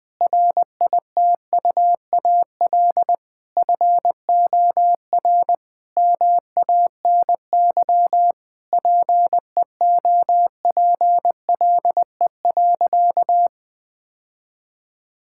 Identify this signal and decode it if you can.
Morse code, heard as a keyed tone, about 21 wpm, 700 Hz. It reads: RITUAL FOR MANY PEOPLE.
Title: Difficult Sentences with Hints - 20wpm